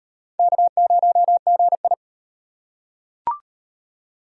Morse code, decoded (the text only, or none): K0GI